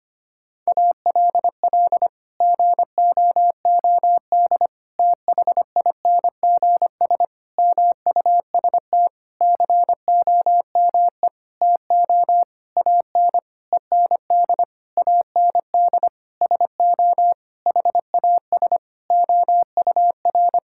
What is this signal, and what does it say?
Morse code, heard as a keyed tone, about 25 wpm, 700 Hz. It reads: ALL GOOD THINGS MUST COME TO AN END AND SO HAS OUR
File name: practice / 25wpm / Sentences with 9-Word Phrases - 25wpm